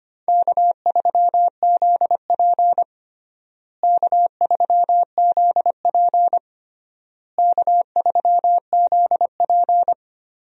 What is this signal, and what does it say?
Morse code, heard as a keyed tone, about 25 wpm, 700 Hz. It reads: K3ZP K3ZP K3ZP
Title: USA Call Signs - 4 Characters - Repeated 3x - 25wpm